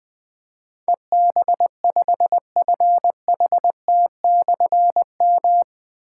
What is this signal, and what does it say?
Morse code, heard as a keyed tone, about 21 wpm, 700 Hz. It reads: EB5FHT/M
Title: Difficult Call Signs 20wpm